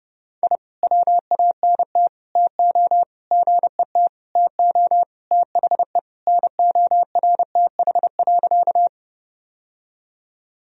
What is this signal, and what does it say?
Morse code, heard as a keyed tone, about 30 wpm, 700 Hz. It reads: I WANT TO GET TO THE NORTH.
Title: Sentences from Top 1000 Words - Review Part 1 - 30wpm